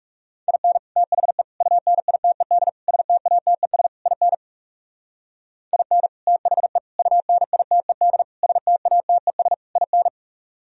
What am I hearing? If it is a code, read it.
Morse code, heard as a keyed tone, about 40 wpm, 700 Hz. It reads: IN THE UNITED STATES IN IN THE UNITED STATES IN